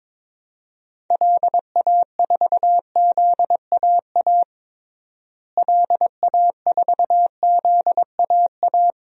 Morse code, heard as a keyed tone, about 22 wpm, 700 Hz. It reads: LA4ZAA LA4ZAA
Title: DXCC-100 Call Signs - 6 Characters - Repeated 2x - 22wpm